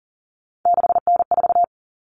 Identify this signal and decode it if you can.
Morse code, heard as a keyed tone, about 40 wpm, 700 Hz. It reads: 6N4
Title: Common Cut Numbers - 3-Digit 40wpm